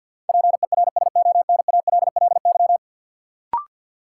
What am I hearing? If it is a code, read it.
Morse code, heard as a keyed tone, about 52 wpm, 700 Hz. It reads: PERSONALLY
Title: Sentences from Top 1400 Words - New Word Drill - 50wpm